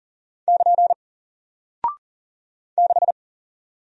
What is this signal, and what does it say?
Morse code, heard as a keyed tone, about 40 wpm, 700 Hz. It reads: <KN> 6
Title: Single Character (Letter:Number:Punctuation:Prosign) Alt 40wpm